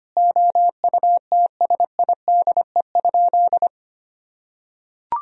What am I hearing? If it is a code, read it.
Morse code, heard as a keyed tone, about 25 wpm, 700 Hz. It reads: OUTSIDE?
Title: Sentences from Top 1000 Words - Review Part 4 - 25wpm